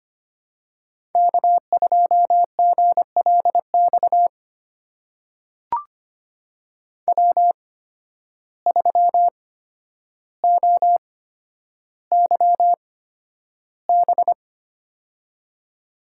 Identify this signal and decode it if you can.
Morse code, heard as a keyed tone, about 25 wpm, spaced as for 11 wpm, 700 Hz. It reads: K2GLX W3OYB
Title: USA Call Signs - 5 Characters - 8x Character Spacing - 25wpm